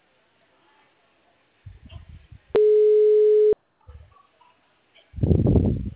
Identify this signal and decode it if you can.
Ringback tone (Europe, 425 Hz).